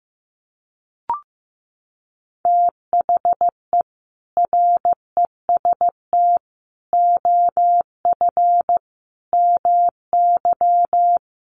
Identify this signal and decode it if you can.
Morse code, heard as a keyed tone, about 15 wpm, 700 Hz. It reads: THE REST OF MY
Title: Sets of 5 Words 15wpm